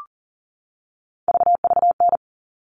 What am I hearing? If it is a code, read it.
Morse code, heard as a keyed tone, about 40 wpm, 700 Hz. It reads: VVN
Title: All Cut Numbers - 3-Digit 40wpm